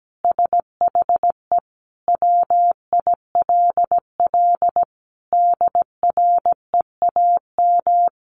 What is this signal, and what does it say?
Morse code, heard as a keyed tone, about 17 wpm, 700 Hz. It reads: SHE WILL DREAM
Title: Sentences from Top 1500 Words 17wpm